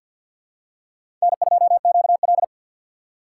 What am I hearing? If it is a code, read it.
Morse code, heard as a keyed tone, about 46 wpm, 700 Hz. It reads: N2QL
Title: USA Call Signs - 4 Characters - 3x Character Spacing - 50wpm